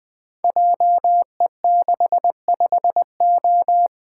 Morse code, heard as a keyed tone, about 20 wpm, 700 Hz. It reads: JE65O
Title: Code Groups - 5 Letter:Numbers - 20wpm